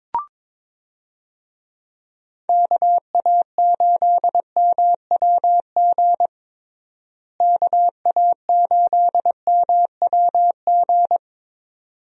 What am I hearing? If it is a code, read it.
Morse code, heard as a keyed tone, about 22 wpm, 700 Hz. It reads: KA8MWG KA8MWG
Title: USA Call Signs - 6 Characters - Repeated 2x - 22wpm